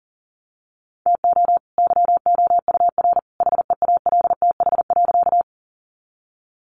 Morse code, heard as a keyed tone, about 40 wpm, 700 Hz. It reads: TO YOUR HEALTH.
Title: Sentences from Top 600 Words 40wpm